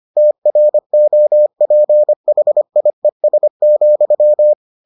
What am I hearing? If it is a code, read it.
Morse code, heard as a keyed tone, about 25 wpm, 600 Hz. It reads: TROPHIES,